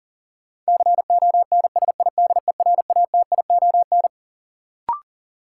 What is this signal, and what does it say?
Morse code, heard as a keyed tone, about 40 wpm, 700 Hz. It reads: CONSIDERATION
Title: Sentences from Top 1500 Words - New Word Drill - 40wpm